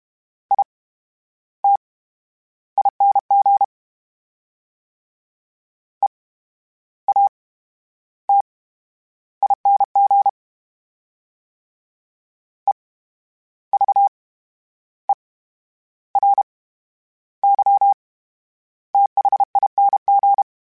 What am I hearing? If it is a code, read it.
Morse code, heard as a keyed tone, about 32 wpm, spaced as for 12 wpm, 800 Hz. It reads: ITING EATING EVERYTHING
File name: Special INGs / Special INGs-mp3s / VerySlowToFaster Ing @30